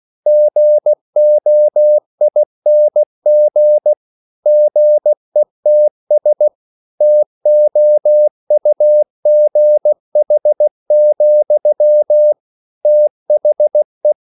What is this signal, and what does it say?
Morse code, heard as a keyed tone, about 16 wpm, 600 Hz. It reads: GOING GETS TOUGH, THE
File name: CW / Practice / EnglishSayings16